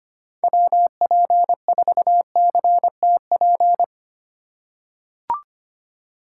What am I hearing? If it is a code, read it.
Morse code, heard as a keyed tone, about 25 wpm, 700 Hz. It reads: WP4CTP
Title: USA Call Signs - 6 Characters - 2x Character Spacing - 25wpm